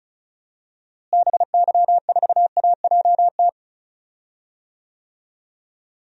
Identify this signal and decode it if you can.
Morse code, heard as a keyed tone, about 34 wpm, 700 Hz. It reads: BY4AJT
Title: DXCC-100 Call Signs - 6 Characters - Repeated 2x - 35wpm